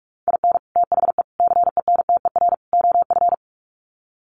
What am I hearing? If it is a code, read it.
Morse code, heard as a keyed tone, about 44 wpm, 700 Hz. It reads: IN THE CENTER OF
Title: Sets of 4 Words 45wpm